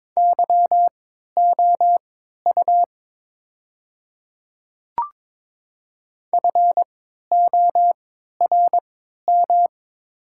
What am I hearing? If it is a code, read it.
Morse code, heard as a keyed tone, about 22 wpm, spaced as for 13 wpm, 700 Hz. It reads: YOU FORM
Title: 3x Character Spacing - Top 200 Words - 22wpm